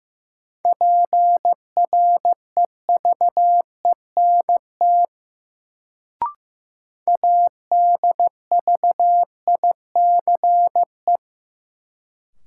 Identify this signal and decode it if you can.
Morse code, heard as a keyed tone, about 15 wpm, 700 Hz. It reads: PREVENT ADVICE
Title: Sentences from Top 1100 Words - New Word Drill - 15wpm